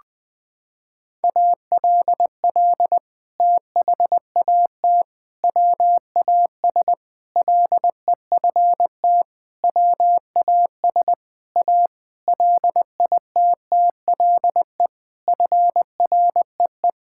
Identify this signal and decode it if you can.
Morse code, heard as a keyed tone, about 20 wpm, 700 Hz. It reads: ALL THAT WAS LEFT WAS A LITTLE FREE
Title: Sentences from Top 1000 Words - Review Part 5 - 20wpm